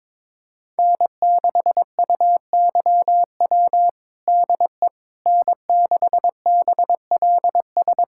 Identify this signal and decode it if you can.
Morse code, heard as a keyed tone, about 22 wpm, 700 Hz. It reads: N6UYW DE N6BLS
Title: USA Callsign DE Callsign - 5 Characters - 22wpm